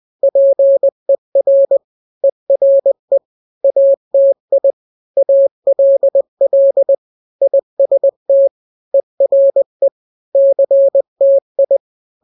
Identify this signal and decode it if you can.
Morse code, heard as a keyed tone, about 20 wpm, 550 Hz. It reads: PER ERE ATI ALL IST ERE CTI
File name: n-grams / 3-grams 20 18 WPM